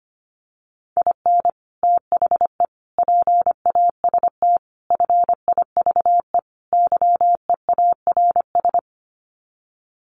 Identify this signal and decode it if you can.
Morse code, heard as a keyed tone, about 25 wpm, 690 Hz. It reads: IN THE PAST FIVE YEARS
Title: Sets of 5 Words Encore 25wpm